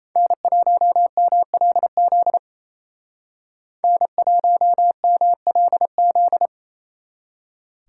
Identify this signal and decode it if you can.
Morse code, heard as a keyed tone, about 36 wpm, 700 Hz. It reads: N1MLZ N1MLZ